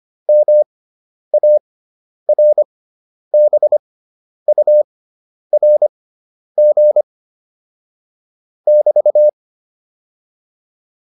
Morse code, heard as a keyed tone, about 25 wpm, spaced as for 10 wpm, 600 Hz. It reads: MARBURG <BT>
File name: QSO405 10